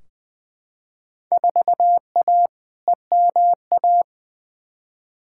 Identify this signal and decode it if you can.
Morse code, heard as a keyed tone, about 21 wpm, 700 Hz. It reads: 4A EMA
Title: Field Day Exchanges - 20wpm